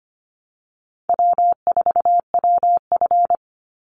Morse code, heard as a keyed tone, about 25 wpm, 700 Hz. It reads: W4WF
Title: Easy Call Signs Repeated 3x - 25wpm